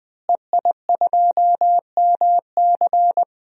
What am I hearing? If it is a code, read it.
Morse code, heard as a keyed tone, about 20 wpm, 700 Hz. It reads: EI2MC
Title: DXCC-100 Call Signs - 5 Characters - Repeated 2x - 20wpm